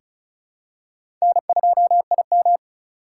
Morse code, heard as a keyed tone, about 34 wpm, 700 Hz. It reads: N2IM
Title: USA Call Signs - 4 Characters - 2x Character Spacing - 35wpm